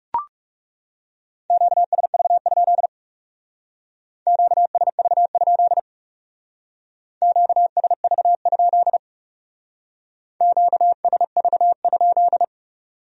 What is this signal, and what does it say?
Morse code, heard as a keyed tone, about 37 wpm, 700 Hz. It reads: QSV? QSV? QSV? QSV?
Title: Speed-Racing Q-Codes - 30wpm